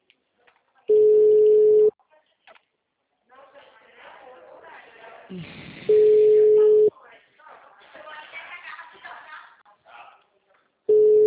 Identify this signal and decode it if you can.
Ringback tone (Europe, 425 Hz), with speech.